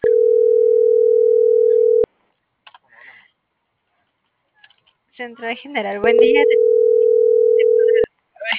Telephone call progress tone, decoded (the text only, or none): ringback tone